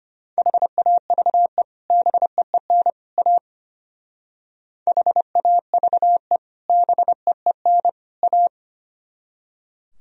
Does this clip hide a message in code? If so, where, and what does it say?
Yes: Morse code from 0.4 to 8.5 s, heard as a keyed tone, about 32 wpm, 700 Hz. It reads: HAVE BEEN A HAVE BEEN A